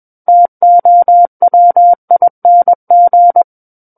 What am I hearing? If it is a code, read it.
Morse code, heard as a keyed tone, about 21 wpm, 700 Hz. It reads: TOWING
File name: Book One / 12-Chapter 12 - 21 WPM